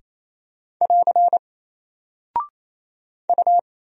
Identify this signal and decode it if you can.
Morse code, heard as a keyed tone, about 27 wpm, 700 Hz. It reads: <AR> U
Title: Single Character (Letter:Number:Punctuation:Prosign) 28wpm